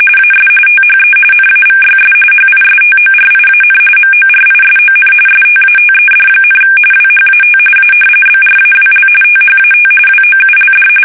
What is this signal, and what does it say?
RTTY, about 45.45 baud. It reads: AS THE NAVYS HISTORIAN MORISON STATED, WHAT A BRAWL THAT WOULD
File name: fsk850 / battleshipnj